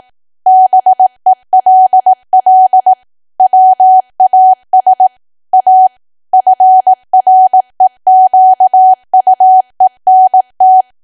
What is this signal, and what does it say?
Morse code, heard as a keyed tone, about 18 wpm, 750 Hz. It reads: BELL WAS A FREQUENT
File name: Archive / 18 WPM / 18WPM